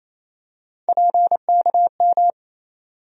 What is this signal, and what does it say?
Morse code, heard as a keyed tone, about 28 wpm, 700 Hz. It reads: PKM
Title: USA Call Sign Postfixes - 3 Letters 28wpm